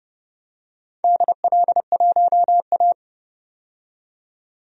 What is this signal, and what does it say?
Morse code, heard as a keyed tone, about 32 wpm, 700 Hz. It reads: DL1A
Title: DXCC-100 Call Signs - 4 Characters - 30wpm